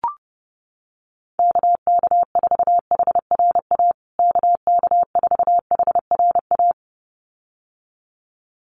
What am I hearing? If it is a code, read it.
Morse code, heard as a keyed tone, about 30 wpm, 700 Hz. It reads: KK4HRA KK4HRA
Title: USA Call Signs Repeated 2x 30wpm